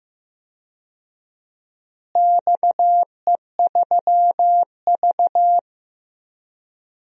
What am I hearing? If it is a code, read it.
Morse code, heard as a keyed tone, about 16 wpm, 700 Hz. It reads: XE3V
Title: DXCC-100 Call Signs - 4 Characters - Repeated 2x - 15wpm